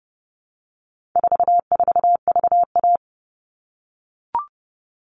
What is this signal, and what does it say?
Morse code, heard as a keyed tone, about 30 wpm, 700 Hz. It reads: 44VA
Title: All Cut Numbers - 4-Digit 30wpm